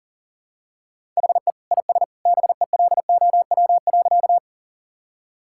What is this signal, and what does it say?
Morse code, heard as a keyed tone, about 37 wpm, 700 Hz. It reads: HE IS BELOW.
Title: Sentences from Top 1000 Words - Review Part 5 - 40wpm